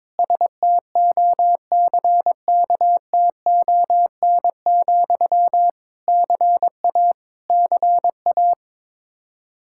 Morse code, heard as a keyed tone, about 22 wpm, 700 Hz. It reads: STOCKTON, CA CA
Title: Practice QSO - Exchange 1 - Beginning - 22wpm